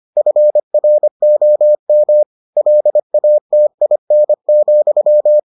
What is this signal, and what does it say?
Morse code, heard as a keyed tone, about 25 wpm, 600 Hz. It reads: FROM LATIN,